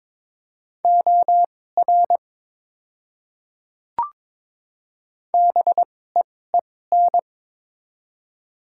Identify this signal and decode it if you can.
Morse code, heard as a keyed tone, about 21 wpm, spaced as for 16 wpm, 690 Hz. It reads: OR BEEN
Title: 2x Character Spacing - Top 100 Words - 22wpm